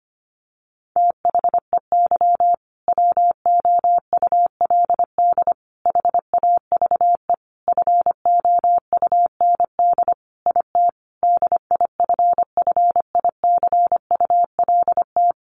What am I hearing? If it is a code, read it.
Morse code, heard as a keyed tone, about 25 wpm, 700 Hz. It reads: THEY WOULD HAVE FOUND IT DIFFICULT
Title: Sentences from Top 1400 Words 25wpm